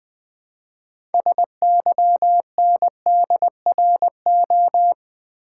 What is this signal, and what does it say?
Morse code, heard as a keyed tone, about 20 wpm, 700 Hz. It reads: SYNDRO
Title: Build A Word - 2 Letters At A Time - 8-Letter Words - 20wpm